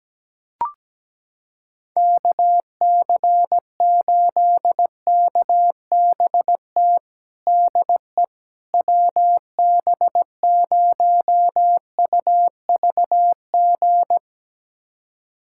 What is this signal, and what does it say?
Morse code, heard as a keyed tone, about 17 wpm, 700 Hz. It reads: KC8KBT DE WB0UVG
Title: USA Callsign DE Callsign - 6 Characters - 17wpm